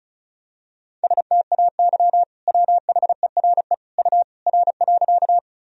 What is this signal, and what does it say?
Morse code, heard as a keyed tone, about 36 wpm, 700 Hz. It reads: STAY WHERE U R.
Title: Sentences from Top 500 Words and Abbreviations 35wpm